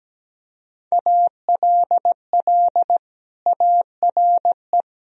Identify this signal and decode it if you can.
Morse code, heard as a keyed tone, about 17 wpm, 700 Hz. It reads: ALL ARE
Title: Sentences from Top 1000 Words - Review Part 3 - 17wpm